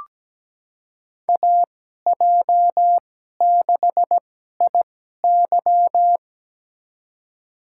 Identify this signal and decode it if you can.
Morse code, heard as a keyed tone, about 17 wpm, spaced as for 12 wpm, 700 Hz. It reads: AJ6IY
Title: USA Call Signs - 5 Characters - 2x Character Spacing - 17wpm